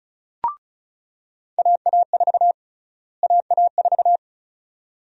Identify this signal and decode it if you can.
Morse code, heard as a keyed tone, about 36 wpm, 700 Hz. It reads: AA4 AA4
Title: USA Call Sign Prefixes - Repeated 2x - 35wpm